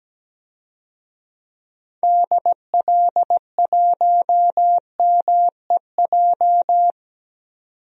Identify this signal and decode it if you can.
Morse code, heard as a keyed tone, about 17 wpm, 700 Hz. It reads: DL1MEJ
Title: DXCC-100 Call Signs - 6 Characters - 17wpm